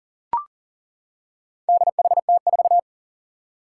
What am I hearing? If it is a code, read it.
Morse code, heard as a keyed tone, about 40 wpm, 700 Hz. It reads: DHT4
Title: Four Challenging Letter-Number Combinations 40wpm